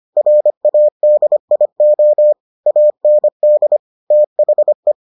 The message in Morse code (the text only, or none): RADIO AND THE